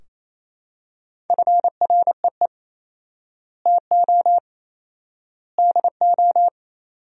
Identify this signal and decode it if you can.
Morse code, heard as a keyed tone, about 28 wpm, 700 Hz. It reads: FREE TO DO
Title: 4x Word-Spacing - Sentences from Top 500 Words - 28wpm